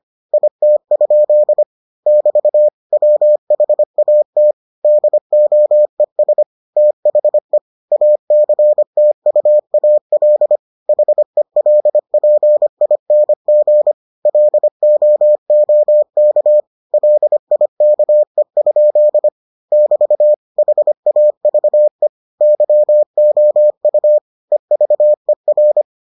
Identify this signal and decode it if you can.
Morse code, heard as a keyed tone, about 25 wpm, 600 Hz. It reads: IT? <BT> WHAT DOES THE ACTUAL HELPING LOOK LIKE? <BT> HAVE YOU EVER